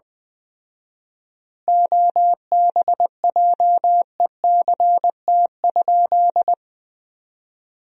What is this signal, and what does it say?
Morse code, heard as a keyed tone, about 20 wpm, 700 Hz. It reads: OBJECT?
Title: 4x Word-Spacing - Sentences from Top 500 Words - 20wpm